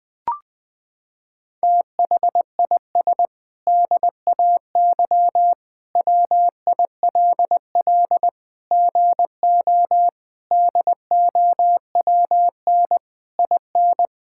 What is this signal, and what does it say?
Morse code, heard as a keyed tone, about 20 wpm, 700 Hz. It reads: THIS DAY WILL GO DOWN IN
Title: Sentences with 8-Word Phrases - 20wpm